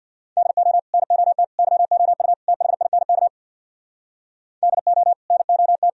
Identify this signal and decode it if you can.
Morse code, heard as a keyed tone, about 64 wpm, 700 Hz. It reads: DO NOT YOU THINK DO NOT